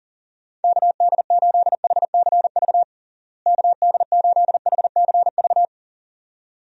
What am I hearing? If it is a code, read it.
Morse code, heard as a keyed tone, about 40 wpm, 700 Hz. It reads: KD8HCV KD8HCV
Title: USA Call Signs - 6 Characters - Repeated 2x - 40wpm